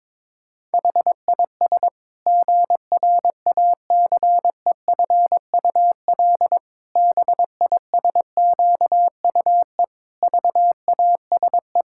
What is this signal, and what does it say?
Morse code, heard as a keyed tone, about 22 wpm, 700 Hz. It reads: HIS GRACEFUL BISQUE VASE
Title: Pangram Challenge v2 - 22wpm